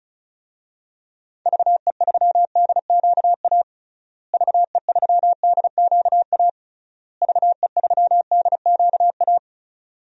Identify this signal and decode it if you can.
Morse code, heard as a keyed tone, about 36 wpm, 700 Hz. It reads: VE3DQA VE3DQA VE3DQA
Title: DXCC-100 Call Signs - 6 Characters - Repeated 3x - 35wpm